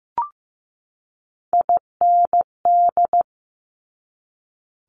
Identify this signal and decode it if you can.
Morse code, heard as a keyed tone, about 15 wpm, 700 Hz. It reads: IND
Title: Multi-letter Phonemes 15wpm